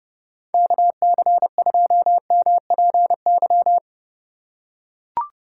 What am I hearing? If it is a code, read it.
Morse code, heard as a keyed tone, about 30 wpm, 700 Hz. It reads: KC2MPY
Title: USA Call Signs - 2x Character Spacing - 30wpm